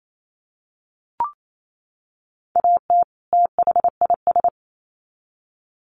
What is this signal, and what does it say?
Morse code, heard as a keyed tone, about 28 wpm, 700 Hz. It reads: AT THIS
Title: Sets of 2 Words 28wpm